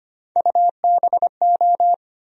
Morse code, heard as a keyed tone, about 25 wpm, 700 Hz. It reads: UBO
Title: USA Call Sign Postfixes - 3 Letters 25wpm